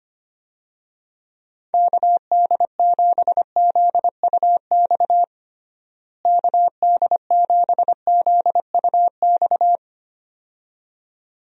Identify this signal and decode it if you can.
Morse code, heard as a keyed tone, about 25 wpm, 700 Hz. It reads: KD7ZUX KD7ZUX
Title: USA Call Signs - 6 Characters - Repeated 2x - 25wpm